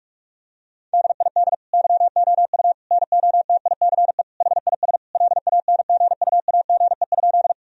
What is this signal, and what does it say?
Morse code, heard as a keyed tone, about 42 wpm, 700 Hz. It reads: DID YOU NOTICE HIS LANGUAGE?
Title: Sentences from Top 1000 Words - Review Part 3 - 45wpm